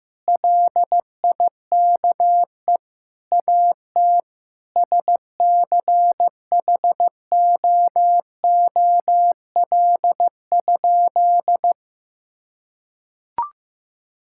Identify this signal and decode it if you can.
Morse code, heard as a keyed tone, about 15 wpm, 700 Hz. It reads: LIKE AT SCHOOL?
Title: Sentences from Top 1000 Words - Review Part 6 - 15wpm